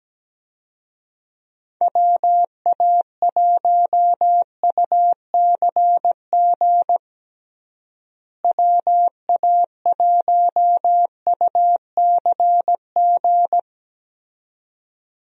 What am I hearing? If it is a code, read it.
Morse code, heard as a keyed tone, about 17 wpm, 700 Hz. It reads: WA1UCG WA1UCG
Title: USA Call Signs - 6 Characters - Repeated 2x - 17wpm